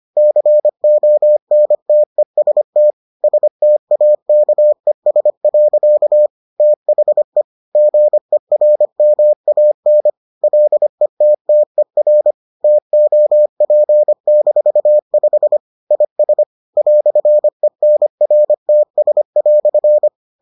Morse code, heard as a keyed tone, about 25 wpm, 600 Hz. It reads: CONTEST STAKES. THE GERMAN LETTER TOP-5 IS "ENRTS"